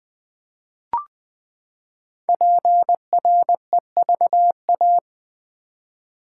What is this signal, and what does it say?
Morse code, heard as a keyed tone, about 20 wpm, 700 Hz. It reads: PREVA
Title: Build A Word - 5 Letters At A Time - 10-Letter Words - 20wpm